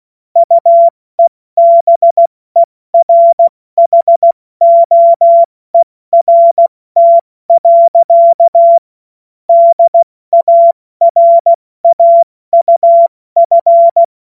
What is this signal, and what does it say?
Morse code, heard as a keyed tone, about 16 wpm, 680 Hz. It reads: UEBERHOERT. DARAUF
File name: Morsekurs / Witze / Witz 30. August 2022 70 F80